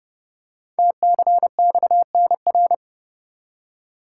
Morse code, heard as a keyed tone, about 29 wpm, 700 Hz. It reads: TCXNR
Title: Code Groups - 5 Letters - 30wpm